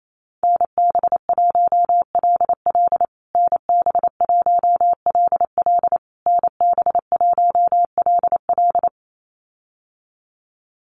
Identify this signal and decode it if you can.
Morse code, heard as a keyed tone, about 28 wpm, 700 Hz. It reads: NB1LL NB1LL NB1LL
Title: USA Call Signs Repeated 3x 28wpm